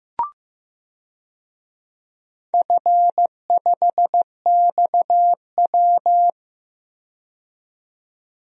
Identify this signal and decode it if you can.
Morse code, heard as a keyed tone, about 15 wpm, 700 Hz. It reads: F5XW
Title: DXCC-100 Call Signs - 4 Characters - 15wpm